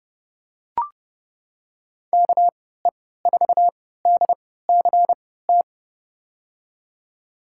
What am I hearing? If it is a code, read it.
Morse code, heard as a keyed tone, about 30 wpm, spaced as for 17 wpm, 700 Hz. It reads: KE4DCT
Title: USA Call Signs - 6 Characters - 3x Character Spacing - 30wpm